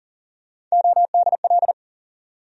Morse code, heard as a keyed tone, about 40 wpm, 700 Hz. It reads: ODL
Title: USA Call Sign Postfixes - 3 Letters 40wpm